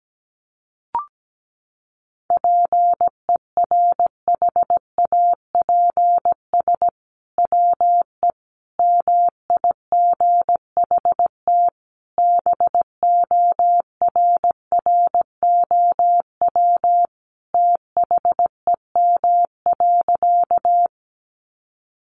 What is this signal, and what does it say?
Morse code, heard as a keyed tone, about 17 wpm, 700 Hz. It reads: PERHAPS WE MIGHT BORROW THEM.
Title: Sentences from Top 1500 Words 17wpm